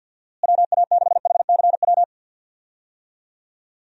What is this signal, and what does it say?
Morse code, heard as a keyed tone, about 49 wpm, 700 Hz. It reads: WA6HCW
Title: USA Call Signs - 6 Characters - Repeated 2x - 50wpm